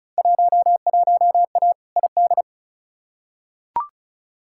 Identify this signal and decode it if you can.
Morse code, heard as a keyed tone, about 36 wpm, 700 Hz. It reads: 11A ID
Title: Field Day Exchanges Encore - 35wpm